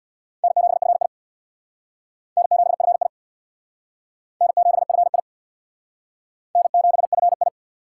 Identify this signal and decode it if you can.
Morse code, heard as a keyed tone, about 60 wpm, 700 Hz. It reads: N7FI N7FI N7FI N7FI
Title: Speed Racing - USA Call Signs 50wpm